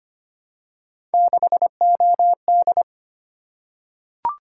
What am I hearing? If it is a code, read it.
Morse code, heard as a keyed tone, about 25 wpm, 700 Hz. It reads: 6OD
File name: Three Letter-Number Combinations - 25wpm